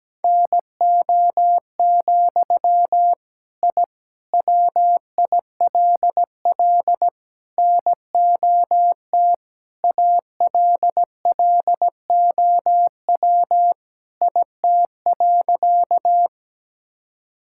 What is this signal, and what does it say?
Morse code, heard as a keyed tone, about 17 wpm, 700 Hz. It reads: NO, I WILL NOT ALLOW IT.
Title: Sentences from Top 1000 Words - Review Part 5 - 17wpm